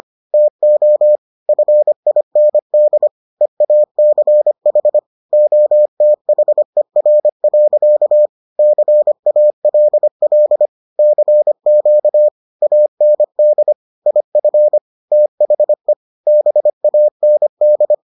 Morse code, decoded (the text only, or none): TO FIND EACH OTHER. CALL CQ AND IF THE BAND